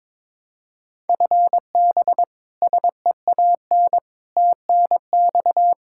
Morse code, heard as a keyed tone, about 22 wpm, 700 Hz. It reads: FB SEAN TNX
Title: Practice QSO - Exchange 3 - More Info - 22wpm